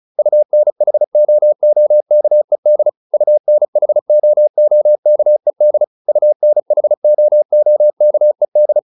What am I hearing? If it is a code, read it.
Morse code, heard as a keyed tone, about 34 wpm, 600 Hz. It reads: UNHOOKED UNHOOKED UNHOOKED